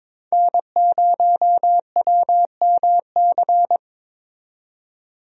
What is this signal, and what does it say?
Morse code, heard as a keyed tone, about 22 wpm, 700 Hz. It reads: N0WMC
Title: USA Call Signs - 5 Characters - 22wpm